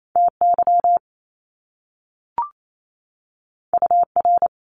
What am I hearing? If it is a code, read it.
Morse code, heard as a keyed tone, about 28 wpm, 700 Hz. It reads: TY UR
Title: Top Two Letter Combinations in Words 28wpm